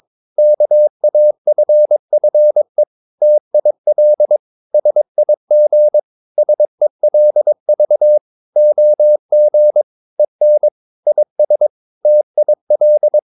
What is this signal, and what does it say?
Morse code, heard as a keyed tone, about 22 wpm, 600 Hz. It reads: KAFFE TIL SIG SELV OG EN IS TIL
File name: Ferie QRS 22wpm